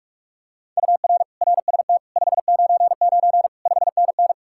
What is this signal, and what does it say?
Morse code, heard as a keyed tone, about 49 wpm, 700 Hz. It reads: UR RST 599 5NN